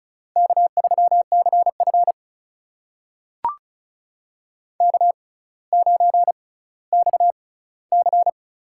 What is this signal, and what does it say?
Morse code, heard as a keyed tone, about 34 wpm, spaced as for 18 wpm, 700 Hz. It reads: K3CF K9XC
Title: USA Call Signs - 4 Characters - 6x Character Spacing - 35wpm